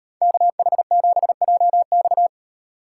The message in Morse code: KH7JX

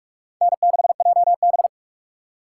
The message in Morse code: N6JB

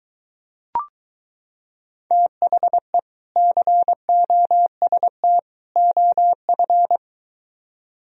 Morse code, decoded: THE COST OF

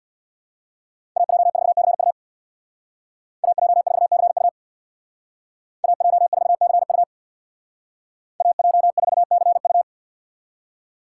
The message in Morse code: AJ4CU AJ4CU AJ4CU AJ4CU

